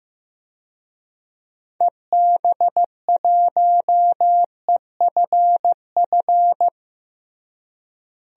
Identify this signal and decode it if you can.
Morse code, heard as a keyed tone, about 15 wpm, 700 Hz. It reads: EB1EFF
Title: DXCC-100 Call Signs - 6 Characters - Repeated 3x - 15wpm